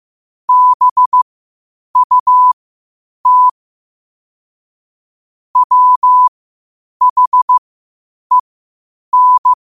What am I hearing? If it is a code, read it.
Morse code, heard as a keyed tone, about 15 wpm, spaced as for 8 wpm, 1000 Hz. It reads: BUT WHEN
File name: qotdfiles / 8wpm